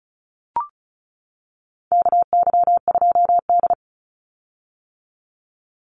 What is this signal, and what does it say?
Morse code, heard as a keyed tone, about 34 wpm, 700 Hz. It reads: KY2D
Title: Call Signs - 35wpm